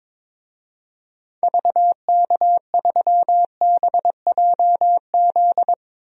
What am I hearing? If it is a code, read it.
Morse code, heard as a keyed tone, about 22 wpm, 700 Hz. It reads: VK3BJZ
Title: DXCC-100 Call Signs - 6 Characters - Repeated 3x - 22wpm